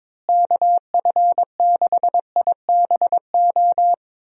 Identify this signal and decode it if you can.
Morse code, heard as a keyed tone, about 22 wpm, 700 Hz. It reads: KF6IBO
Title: USA Call Signs - 6 Characters - Repeated 2x - 22wpm